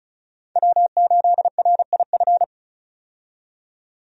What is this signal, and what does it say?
Morse code, heard as a keyed tone, about 34 wpm, 700 Hz. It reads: W8RIF
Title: USA Call Signs - 5 Characters - 35wpm